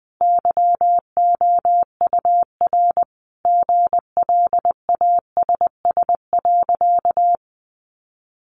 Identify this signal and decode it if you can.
Morse code, heard as a keyed tone, about 19 wpm, 700 Hz. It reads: YOUR GLASS.
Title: Sentences from Top 1300 Words 20wpm